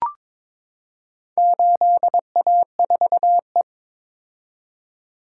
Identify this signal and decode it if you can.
Morse code, heard as a keyed tone, about 22 wpm, 700 Hz. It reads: 8A4E
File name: practice / 22wpm / Four Letter-Number Combinations - 22wpm